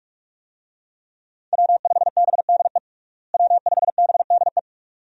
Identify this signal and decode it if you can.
Morse code, heard as a keyed tone, about 46 wpm, 700 Hz. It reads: W5BDE W5BDE
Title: USA Call Signs - 5 Characters - Repeated 2x - 45wpm